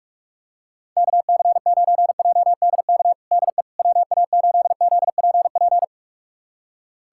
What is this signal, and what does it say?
Morse code, heard as a keyed tone, about 46 wpm, 700 Hz. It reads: KC9JDK DE WA8ZPP
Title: USA Callsign DE Callsign - 6 Characters - 45wpm